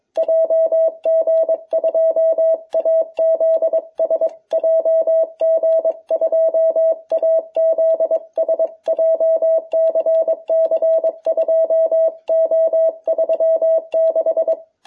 Morse code, heard as a keyed tone, about 23 wpm, 660 Hz. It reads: JG2AZS JG2AZS JCC 2O36